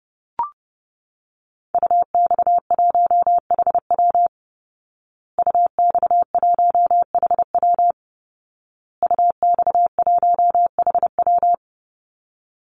Morse code, heard as a keyed tone, about 28 wpm, 700 Hz. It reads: UX1HW UX1HW UX1HW
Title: Call Signs Repeated 2x - 30wpm